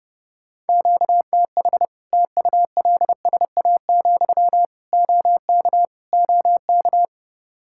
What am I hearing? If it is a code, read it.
Morse code, heard as a keyed tone, about 30 wpm, 700 Hz. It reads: QTH TULSA, OK OK
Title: Practice QSO - Exchange 1 - Beginning - 30wpm